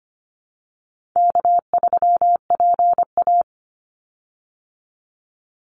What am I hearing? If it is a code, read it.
Morse code, heard as a keyed tone, about 25 wpm, 700 Hz. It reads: K3PA
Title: Easy Call Signs Repeated 3x - 25wpm